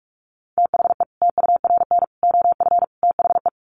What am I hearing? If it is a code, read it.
Morse code, heard as a keyed tone, about 44 wpm, 700 Hz. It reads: THE TURN OF THE
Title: Sets of 4 Words 45wpm